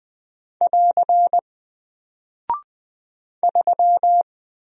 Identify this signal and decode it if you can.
Morse code, heard as a keyed tone, about 20 wpm, 700 Hz. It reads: <AR> 3
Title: Single Character (Letter:Number:Punctuation:Prosign) Alt 20wpm